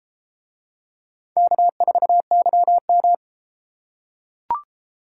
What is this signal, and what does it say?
Morse code, heard as a keyed tone, about 30 wpm, 700 Hz. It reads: K4YM